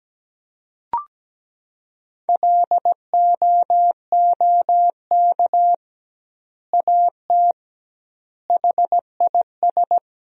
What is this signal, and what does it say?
Morse code, heard as a keyed tone, about 17 wpm, 700 Hz. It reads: LOOK AT HIS
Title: 2x Word-Spacing - Sentences from Top 300 Words - 17wpm